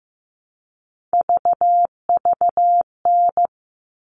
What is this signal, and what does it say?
Morse code, heard as a keyed tone, about 15 wpm, 700 Hz. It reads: VVN
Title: All Cut Numbers - 3-Digit 15wpm